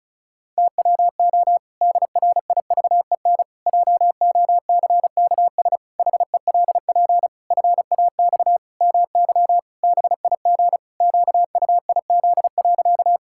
Morse code, heard as a keyed tone, about 34 wpm, 700 Hz. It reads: TWO DRIVEN JOCKS HELP FAX MY BIG QUIZ.
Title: Pangram Challenge v2 - 35wpm